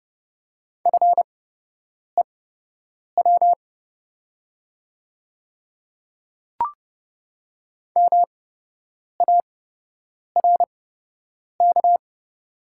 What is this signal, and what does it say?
Morse code, heard as a keyed tone, about 30 wpm, spaced as for 8 wpm, 700 Hz. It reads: FEW MARK
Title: 8x Character Spacing - Top 500 Words - 30wpm